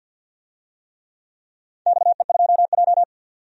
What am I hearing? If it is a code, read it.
Morse code, heard as a keyed tone, about 47 wpm, 700 Hz. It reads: XE2J
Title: DXCC-100 Call Signs - 4 Characters - 50wpm